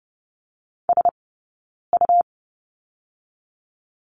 Morse code, heard as a keyed tone, about 30 wpm, 700 Hz. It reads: S U
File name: Lesson 18d - Letter m - Two Characters 30wpm